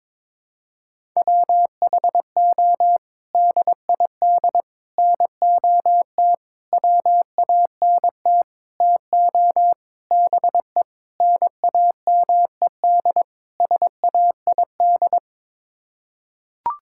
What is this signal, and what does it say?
Morse code, heard as a keyed tone, about 22 wpm, 700 Hz. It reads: WHO DID NOT WANT TO BE NAMED SAID
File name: practice / 22wpm / Sets of 8 Words - 22wpm